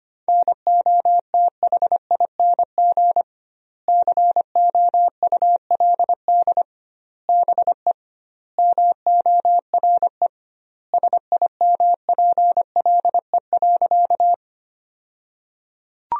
Morse code, heard as a keyed tone, about 25 wpm, 700 Hz. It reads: NOTHING COULD BE MORE SIMPLE.
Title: 2x Word-Spacing - Sentences from Top 500 Words - 25wpm